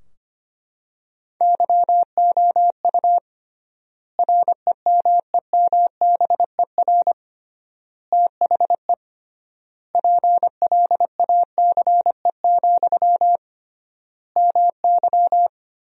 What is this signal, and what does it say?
Morse code, heard as a keyed tone, about 25 wpm, 700 Hz. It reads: YOU REMEMBER THE PLACE, MY
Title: 3x Word-Spacing - Sentences from Top 400 Words - 25wpm